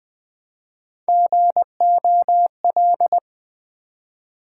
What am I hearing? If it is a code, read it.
Morse code, heard as a keyed tone, about 20 wpm, 700 Hz. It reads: GOL
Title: Build A Word - 1 Letter At A Time - 4-Letter Words - 20wpm